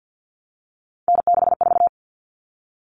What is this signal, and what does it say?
Morse code, heard as a keyed tone, about 50 wpm, 700 Hz. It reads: N64
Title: Common Cut Numbers - 3-Digit 50wpm